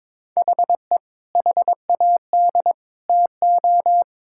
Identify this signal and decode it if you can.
Morse code, heard as a keyed tone, about 22 wpm, 700 Hz. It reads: HE HAD TO
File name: practice / 22wpm / Sentences with 3-Word Phrases - 22wpm